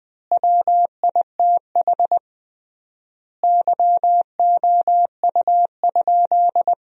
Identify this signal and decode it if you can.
Morse code, heard as a keyed tone, about 20 wpm, 700 Hz. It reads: WITH YOU?
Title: 3x Word-Spacing - Sentences from Top 100 Words - 20wpm